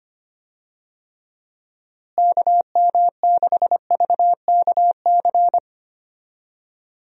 Morse code, heard as a keyed tone, about 25 wpm, 700 Hz. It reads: KM6VKC
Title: USA Call Signs - 6 Characters - Repeated 2x - 25wpm